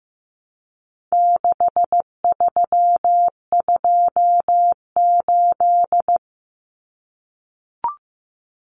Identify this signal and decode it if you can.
Morse code, heard as a keyed tone, about 15 wpm, 700 Hz. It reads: 6328
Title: Common Cut Numbers - 4-Digit 15wpm